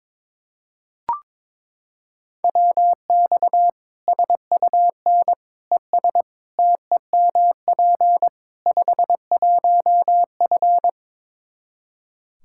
Morse code, heard as a keyed tone, about 22 wpm, 700 Hz. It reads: WX SUN ES TEMP 51F
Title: Practice QSO - Exchange 2 - Info - 22wpm